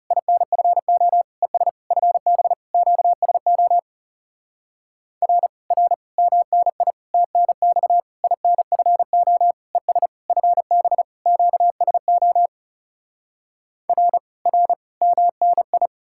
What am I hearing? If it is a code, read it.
Morse code, heard as a keyed tone, about 40 wpm, 700 Hz. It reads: INFO ES FB QSO R R MNI TNX INFO ES FB QSO R R MNI